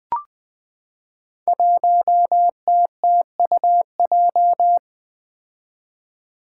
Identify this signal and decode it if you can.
Morse code, heard as a keyed tone, about 20 wpm, 700 Hz. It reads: 1TTUJ
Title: Five Challenging Letter-Number Combinations 20wpm